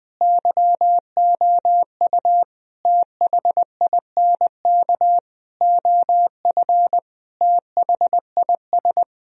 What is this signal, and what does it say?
Morse code, heard as a keyed tone, about 21 wpm, 700 Hz. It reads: YOU THINK OF THIS
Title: Sentences from Top 1000 Words - Review Part 2 - 20wpm